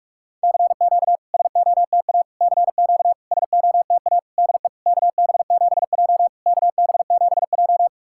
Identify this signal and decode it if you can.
Morse code, heard as a keyed tone, about 44 wpm, 690 Hz. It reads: CQ SOTA CQ SOTA DE KB7J KB7J
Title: SOTA QSO Practice 45wpm